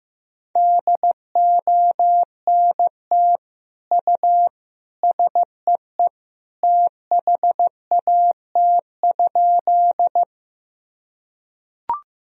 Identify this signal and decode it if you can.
Morse code, heard as a keyed tone, about 16 wpm, 700 Hz. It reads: DONT U SEE THAT?
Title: Sentences from Top 300 Words and Abbreviations 15wpm